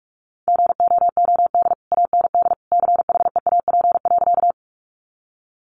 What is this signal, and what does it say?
Morse code, heard as a keyed tone, about 44 wpm, 700 Hz. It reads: GOOD AND CHEAP.
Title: Sentences from Top 1200 Words 45wpm